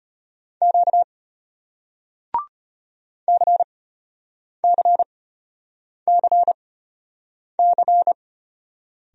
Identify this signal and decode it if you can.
Morse code, heard as a keyed tone, about 34 wpm, spaced as for 8 wpm, 700 Hz. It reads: Q CCCC